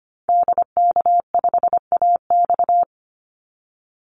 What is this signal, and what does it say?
Morse code, heard as a keyed tone, about 25 wpm, 700 Hz. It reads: DK5AX
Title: Call Signs Repeated 2x - 25wpm